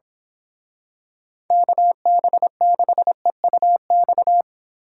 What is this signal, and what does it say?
Morse code, heard as a keyed tone, about 26 wpm, 700 Hz. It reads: KB6EUX
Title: Speed Racing - USA 2x3 Call Signs 22wpm